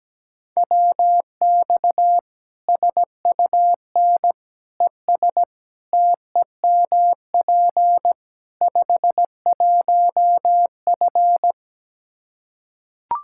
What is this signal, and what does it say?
Morse code, heard as a keyed tone, about 17 wpm, 700 Hz. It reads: WX SUN ES TEMP 51F
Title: Practice QSO - Exchange 2 - Info - 17wpm